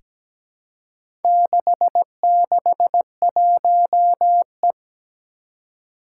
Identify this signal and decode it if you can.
Morse code, heard as a keyed tone, about 17 wpm, 700 Hz. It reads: 661E
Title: Four Challenging Letter-Number Combinations 17wpm